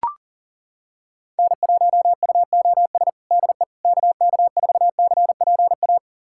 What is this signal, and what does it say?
Morse code, heard as a keyed tone, about 40 wpm, 700 Hz. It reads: N1UOS DE KK4CPA
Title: USA Callsign DE Callsign 40wpm